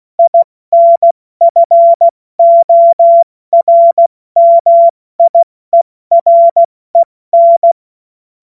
Morse code, heard as a keyed tone, about 16 wpm, 680 Hz. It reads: INFORMIEREN
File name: Morsekurs / Witze / Witz 30 Januar 2024 70 F80